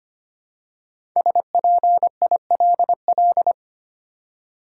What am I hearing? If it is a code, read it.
Morse code, heard as a keyed tone, about 25 wpm, 700 Hz. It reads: SPILL
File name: 5 Letter Words - First 3 Letters Given - 25wpm